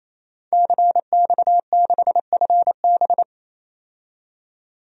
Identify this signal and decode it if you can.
Morse code, heard as a keyed tone, about 28 wpm, 700 Hz. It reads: CX6FB
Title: DXCC-100 Call Signs - 5 Characters - 28wpm